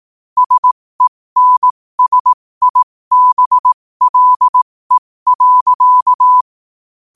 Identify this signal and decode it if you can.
Morse code, heard as a keyed tone, about 18 wpm, spaced as for 15 wpm, 1000 Hz. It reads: SENSIBLE.